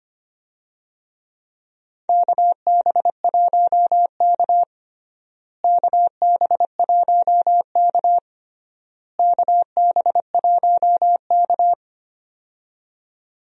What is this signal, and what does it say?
Morse code, heard as a keyed tone, about 25 wpm, 700 Hz. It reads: KB1K KB1K KB1K
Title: USA Call Signs - 4 Characters - Repeated 3x - 25wpm